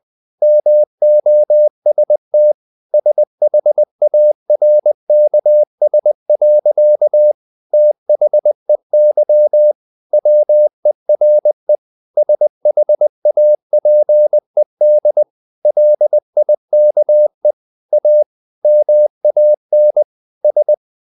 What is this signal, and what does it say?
Morse code, heard as a keyed tone, about 20 wpm, 600 Hz. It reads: MOST SHARKS. THEY WERE SHAPED LIKE A MAN S
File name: the old man and the sea part 2 20wpm